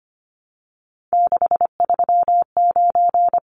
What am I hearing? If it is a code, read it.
Morse code, heard as a keyed tone, about 25 wpm, 700 Hz. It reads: 639
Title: Numbers - 3-Digit 25wpm